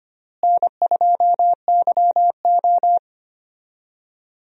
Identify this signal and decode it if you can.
Morse code, heard as a keyed tone, about 25 wpm, 700 Hz. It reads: N2YO
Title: USA Call Signs - 4 Characters - Repeated 3x - 25wpm